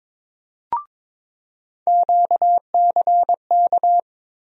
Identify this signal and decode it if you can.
Morse code, heard as a keyed tone, about 22 wpm, 700 Hz. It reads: QCK
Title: USA Call Sign Postfixes - 3 Letters 22wpm